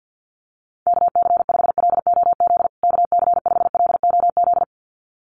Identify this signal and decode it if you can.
Morse code, heard as a keyed tone, about 48 wpm, 700 Hz. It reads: KC5LOZ KC5LOZ
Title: USA Call Signs Repeated 2x 50wpm